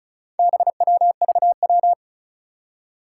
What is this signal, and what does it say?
Morse code, heard as a keyed tone, about 36 wpm, 700 Hz. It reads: BWVW
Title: Four Challenging Letter-Number Combinations 35wpm